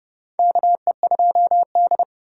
Morse code, heard as a keyed tone, about 31 wpm, 700 Hz. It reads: KE2D
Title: K1USN SST QSO Practice - 30wpm